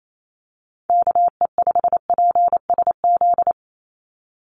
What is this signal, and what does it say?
Morse code, heard as a keyed tone, about 28 wpm, 700 Hz. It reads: KE5PSZ
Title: USA Call Signs 28wpm